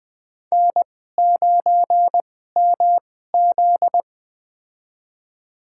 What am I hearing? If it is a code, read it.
Morse code, heard as a keyed tone, about 20 wpm, spaced as for 14 wpm, 700 Hz. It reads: N9MZ
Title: USA Call Signs - 4 Characters - 2x Character Spacing - 20wpm